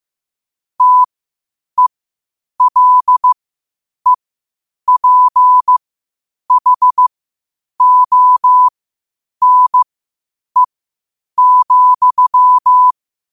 Morse code, heard as a keyed tone, about 15 wpm, spaced as for 8 wpm, 1000 Hz. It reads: TELEPHONE,
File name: qotdfiles / 8wpm